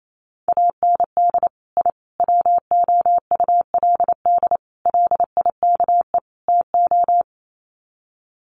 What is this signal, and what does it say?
Morse code, heard as a keyed tone, about 28 wpm, 700 Hz. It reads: AND I WOULD LIKE TO
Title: Sets of 5 Words Encore 28wpm